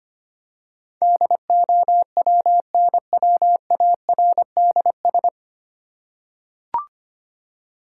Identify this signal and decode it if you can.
Morse code, heard as a keyed tone, about 25 wpm, 700 Hz. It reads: DOWNWARDS
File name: Nine Letter Words - 25wpm